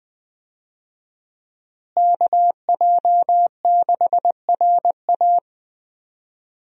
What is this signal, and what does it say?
Morse code, heard as a keyed tone, about 20 wpm, 700 Hz. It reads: KJ6RA
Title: USA Call Signs - 5 Characters - Repeated 3x - 20wpm